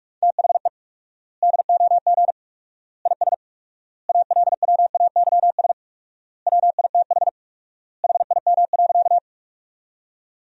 Morse code, heard as a keyed tone, about 44 wpm, 700 Hz. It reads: THE DOG IS ALWAYS WITH HIM.
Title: 4x Word-Spacing - Sentences from Top 400 Words - 45wpm